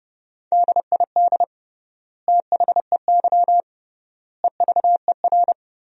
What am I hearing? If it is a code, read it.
Morse code, heard as a keyed tone, about 30 wpm, 700 Hz. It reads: DID THEY EVER
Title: 3x Word-Spacing - Sentences from Top 400 Words - 30wpm